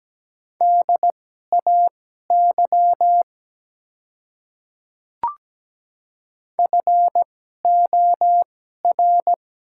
Morse code, heard as a keyed tone, about 17 wpm, spaced as for 12 wpm, 700 Hz. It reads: DAY FOR
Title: 2x Character Spacing - Top 100 Words - 17wpm